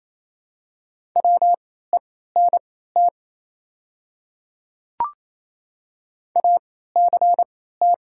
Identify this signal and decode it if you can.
Morse code, heard as a keyed tone, about 28 wpm, spaced as for 16 wpm, 700 Hz. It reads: WENT ACT
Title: 3x Character Spacing - Top 200 Words - 28wpm